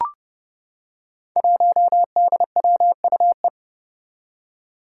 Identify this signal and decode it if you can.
Morse code, heard as a keyed tone, about 30 wpm, 700 Hz. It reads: 1DWUE
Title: Five Challenging Letter-Number Combinations 30wpm